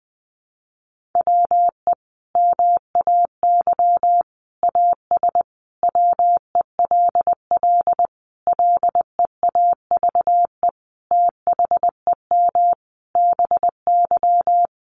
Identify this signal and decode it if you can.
Morse code, heard as a keyed tone, about 20 wpm, 700 Hz. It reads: WE MAY AS WELL LEAVE THEM BY
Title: Sentences from Top 800 Words 20wpm